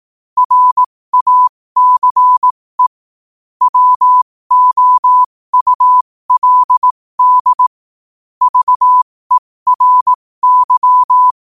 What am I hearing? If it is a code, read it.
Morse code, heard as a keyed tone, about 18 wpm, spaced as for 15 wpm, 1000 Hz. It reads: RACE WOULD VERY
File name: qotdfiles / 15wpm